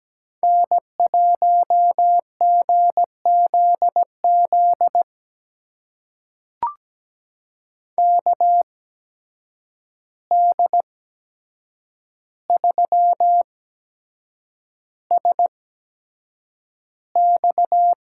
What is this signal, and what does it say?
Morse code, heard as a keyed tone, about 17 wpm, spaced as for 7 wpm, 700 Hz. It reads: N1GZZ KD3SX
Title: USA Call Signs - 5 Characters - 8x Character Spacing - 17wpm